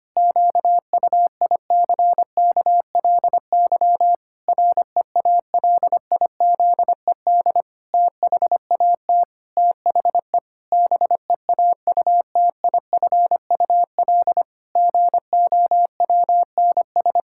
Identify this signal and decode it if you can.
Morse code, heard as a keyed tone, about 25 wpm, 700 Hz. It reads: QUICKLY REALIZED THAT THE BEAUTIFUL GOWNS
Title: Pangram Challenge v2 - 25wpm